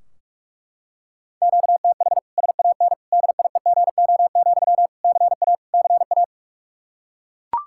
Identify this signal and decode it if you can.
Morse code, heard as a keyed tone, about 42 wpm, 700 Hz. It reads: QTH SAN DIEGO, CA CA
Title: Practice QSO - Exchange 1 - Beginning - 45wpm